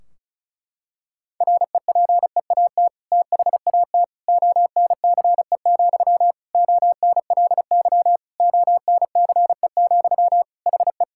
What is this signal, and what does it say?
Morse code, heard as a keyed tone, about 36 wpm, 700 Hz. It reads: REPEAT THAT ONCE, ONLY ONCE, HE
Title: Sentences from Top 1600 Words 35wpm